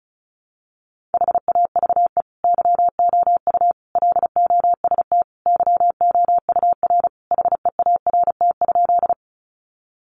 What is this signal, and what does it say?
Morse code, heard as a keyed tone, about 34 wpm, 700 Hz. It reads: HAVE YOU LOST YOUR HEART?
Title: Sentences from Top 1400 Words 35wpm